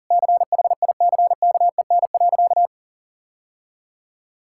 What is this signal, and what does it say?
Morse code, heard as a keyed tone, about 37 wpm, 700 Hz. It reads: CHICKEN.
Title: Sentences from Top 1000 Words - Review Part 5 - 40wpm